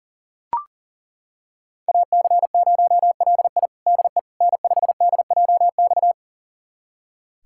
Morse code, heard as a keyed tone, about 40 wpm, 700 Hz. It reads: AC0LI DE N5DJX
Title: USA Callsign DE Callsign 40wpm